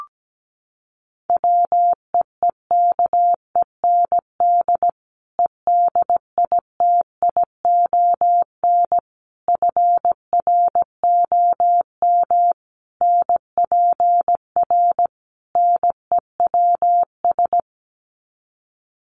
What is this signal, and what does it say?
Morse code, heard as a keyed tone, about 17 wpm, 700 Hz. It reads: WEEKEND EDITION FROM NPR NEWS
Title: Sets of 5 Words 17wpm